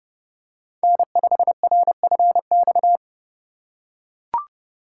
Morse code, heard as a keyed tone, about 30 wpm, 700 Hz. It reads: N5RFX
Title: USA Call Signs - 5 Characters - 2x Character Spacing - 30wpm